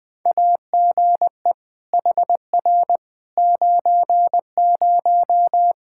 Morse code, heard as a keyed tone, about 20 wpm, 700 Hz. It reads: AGE HR 90
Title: Practice QSO - Exchange 3 - More Info - 20wpm